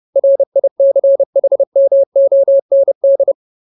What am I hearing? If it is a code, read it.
Morse code, heard as a keyed tone, about 30 wpm, 550 Hz. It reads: RICHMOND
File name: Ultimate Word List / 30WPM Ultimate